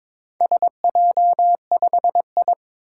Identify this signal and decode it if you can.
Morse code, heard as a keyed tone, about 22 wpm, 700 Hz. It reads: SJ5I
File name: practice / 22wpm / Four Letter-Number Combinations - 22wpm